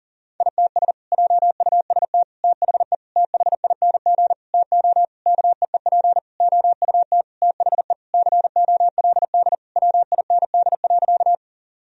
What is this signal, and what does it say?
Morse code, heard as a keyed tone, about 40 wpm, 700 Hz. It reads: ITS JUST THE THING TO KEEP OUT THE COLD WIND.
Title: Sentences from Top 1000 Words - Review Part 2 - 40wpm